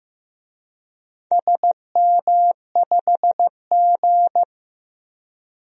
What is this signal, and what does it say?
Morse code, heard as a keyed tone, about 15 wpm, 700 Hz. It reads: SM5G
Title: DXCC-100 Call Signs - 4 Characters - Repeated 2x - 15wpm